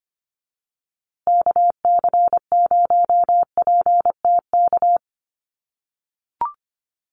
Morse code, heard as a keyed tone, about 25 wpm, 700 Hz. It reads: KC0PTK
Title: USA Call Signs Repeated 2x 25wpm